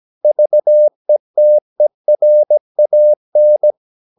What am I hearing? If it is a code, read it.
Morse code, heard as a keyed tone, about 17 wpm, 600 Hz. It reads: VETERAN
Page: Romanen Never af Ken Follet på engelsk.